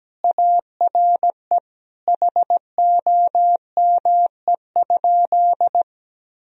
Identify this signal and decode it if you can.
Morse code, heard as a keyed tone, about 17 wpm, 700 Hz. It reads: ARE HOME?
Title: Sentences from Top 1000 Words - Review Part 1 - 17wpm